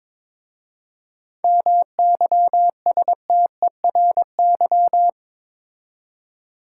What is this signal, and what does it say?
Morse code, heard as a keyed tone, about 22 wpm, 700 Hz. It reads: MYSTERY
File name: Difficult Words with Hints - 22wpm